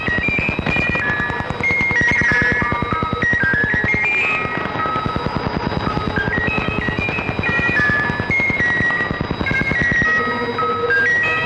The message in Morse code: none